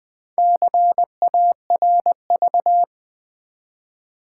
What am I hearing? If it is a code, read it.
Morse code, heard as a keyed tone, about 20 wpm, 700 Hz. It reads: CARV
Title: Build A Word - 1 Letter At A Time - 5-Letter Words - 20wpm